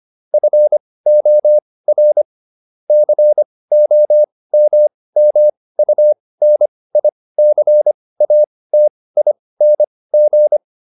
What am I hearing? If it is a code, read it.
Morse code, heard as a keyed tone, about 25 wpm, spaced as for 18 wpm, 600 Hz. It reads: FOR COMMUNICATING